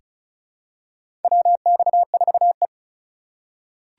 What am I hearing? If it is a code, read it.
Morse code, heard as a keyed tone, about 36 wpm, 700 Hz. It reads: WX4E
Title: USA Call Signs - 4 Characters - 3x Character Spacing - 35wpm